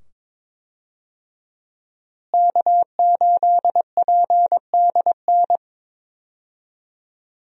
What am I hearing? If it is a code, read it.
Morse code, heard as a keyed tone, about 22 wpm, 700 Hz. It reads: K8PDN
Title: USA Call Signs - 5 Characters - Repeated 2x - 22wpm